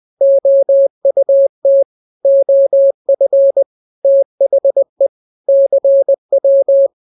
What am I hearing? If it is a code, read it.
Morse code, heard as a keyed tone, about 20 wpm, 550 Hz. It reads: OUT OF THE CW
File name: jokes / joke 3 20 0000